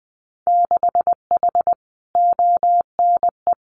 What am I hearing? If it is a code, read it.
Morse code, heard as a keyed tone, about 20 wpm, 700 Hz. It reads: 6H ONE
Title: Winter Field Day Exchanges Encore 20wpm